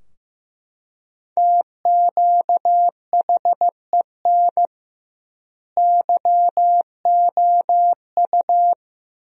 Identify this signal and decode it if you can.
Morse code, heard as a keyed tone, about 15 wpm, 700 Hz. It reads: TQHEN YOU
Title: 2x Word-Spacing - Sentences from Top 100 Words - 15wpm